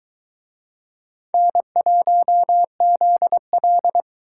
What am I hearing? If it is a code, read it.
Morse code, heard as a keyed tone, about 24 wpm, 700 Hz. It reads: N1ZL